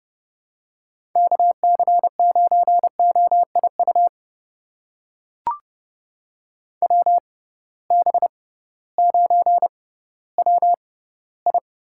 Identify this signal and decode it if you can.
Morse code, heard as a keyed tone, about 30 wpm, spaced as for 16 wpm, 700 Hz. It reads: KC9OIU WB9WI
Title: USA Call Signs - 6 Characters - 6x Character Spacing - 30wpm